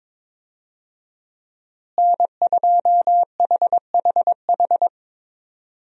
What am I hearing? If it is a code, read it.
Morse code, heard as a keyed tone, about 22 wpm, 700 Hz. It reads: N2HHH
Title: USA Call Signs - 5 Characters - Repeated 2x - 22wpm